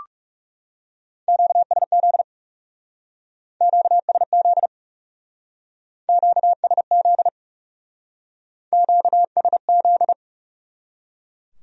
Morse code, heard as a keyed tone, about 40 wpm, 700 Hz. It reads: QSZ QSZ QSZ QSZ